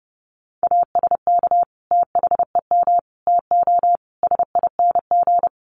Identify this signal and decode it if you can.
Morse code, heard as a keyed tone, about 29 wpm, 700 Hz. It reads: ASK THEM TO SING
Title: Sentences from Top 400 Words 30wpm